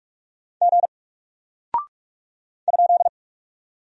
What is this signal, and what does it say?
Morse code, heard as a keyed tone, about 44 wpm, 700 Hz. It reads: G ?
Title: Single Character (Letter:Number:Punctuation:Prosign) Alt 45wpm